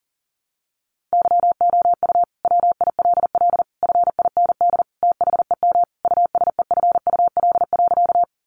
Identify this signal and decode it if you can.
Morse code, heard as a keyed tone, about 40 wpm, 700 Hz. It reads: YOU WILL FIND THEM USEFUL.
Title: Sentences from Top 700 Words 40wpm